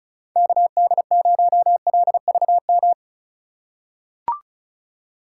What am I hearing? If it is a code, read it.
Morse code, heard as a keyed tone, about 36 wpm, 700 Hz. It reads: KD0LVM
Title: USA Call Signs - 6 Characters - 4x Character Spacing - 35wpm